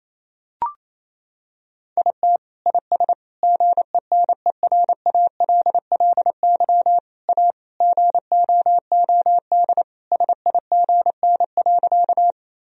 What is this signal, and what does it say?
Morse code, heard as a keyed tone, about 28 wpm, 700 Hz. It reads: IT IS GENERALLY A GOOD SIGN.
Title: Sentences from Top 1000 Words - Review Part 4 - 28wpm